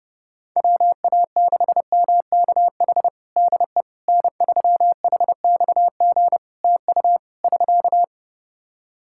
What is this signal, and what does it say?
Morse code, heard as a keyed tone, about 30 wpm, 700 Hz. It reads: WA6MKH DE N3HXG TU <SK>
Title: Practice QSO - Exchange 4 - Ending - 30wpm